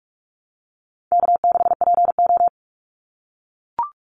Morse code, heard as a keyed tone, about 44 wpm, 700 Hz. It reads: K6PO
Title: Easy Call Signs - 45wpm